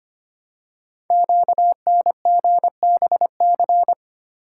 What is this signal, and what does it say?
Morse code, heard as a keyed tone, about 25 wpm, 700 Hz. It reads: QNGBC
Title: Code Groups - 5 Letters - 25wpm